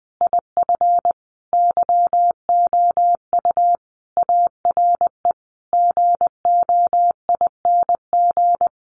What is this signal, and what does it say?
Morse code, heard as a keyed tone, about 20 wpm, 700 Hz. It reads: IF YOU ARE GOING
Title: Sentences from Top 500 Words 20wpm